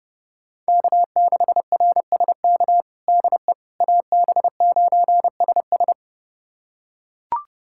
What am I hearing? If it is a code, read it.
Morse code, heard as a keyed tone, about 30 wpm, 700 Hz. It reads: K6RSK DE AB9SS
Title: USA Callsign DE Callsign - 5 Characters - 30wpm